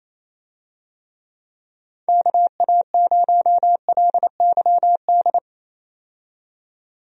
Morse code, heard as a keyed tone, about 27 wpm, 700 Hz. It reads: KA0LYD
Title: USA Call Signs - 6 Characters - 28wpm